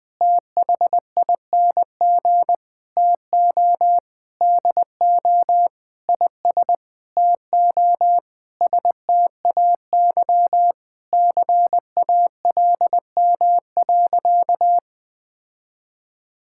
Morse code, heard as a keyed tone, about 21 wpm, 700 Hz. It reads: THING TO DO IS TO STAY CALM.
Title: Sentences with 7-Word Phrases - 20wpm